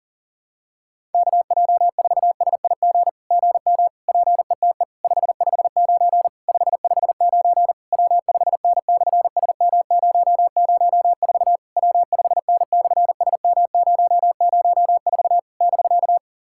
Morse code, heard as a keyed tone, about 37 wpm, 690 Hz. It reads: KJ4SIG GM PETE 559 559 W5N/SM004 W5N/SM004 <BK>
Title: SOTA QSO Practice 40wpm